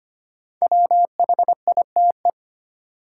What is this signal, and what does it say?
Morse code, heard as a keyed tone, about 25 wpm, 700 Hz. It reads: WHITE
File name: 5 Letter Words - First 3 Letters Given - 25wpm